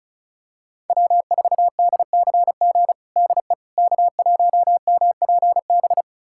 Morse code, heard as a keyed tone, about 34 wpm, 700 Hz. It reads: W4DCG DE K1MPB
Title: USA Callsign DE Callsign - 5 Characters - 35wpm